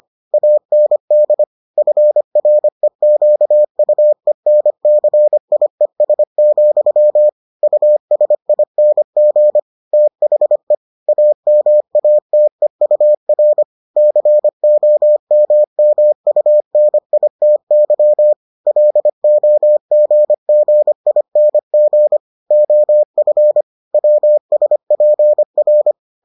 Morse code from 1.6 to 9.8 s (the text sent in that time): FREQUENCIES, USING